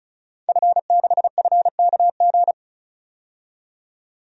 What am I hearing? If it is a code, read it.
Morse code, heard as a keyed tone, about 36 wpm, 700 Hz. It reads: F6FKG
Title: DXCC-100 Call Signs - 5 Characters - 35wpm